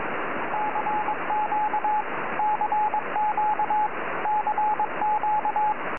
Morse code, heard as a keyed tone, about 21 wpm, 850 Hz, with radio noise. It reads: CQ CQ CQ
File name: es2jl 4m